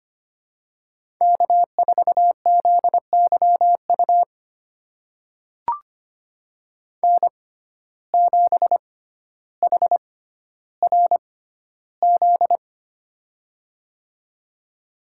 Morse code, heard as a keyed tone, about 25 wpm, spaced as for 13 wpm, 700 Hz. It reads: K4ZYU N7HRZ
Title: USA Call Signs - 5 Characters - 6x Character Spacing - 25wpm